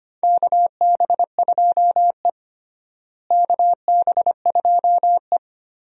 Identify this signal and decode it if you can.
Morse code, heard as a keyed tone, about 25 wpm, 700 Hz. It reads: KB2E KB2E
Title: USA Call Signs - 4 Characters - Repeated 3x - 25wpm